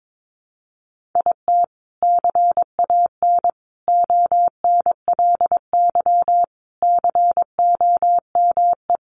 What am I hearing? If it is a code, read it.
Morse code, heard as a keyed tone, about 22 wpm, 700 Hz. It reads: IT CAN ONLY COME
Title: Sentences from Top 1400 Words 22wpm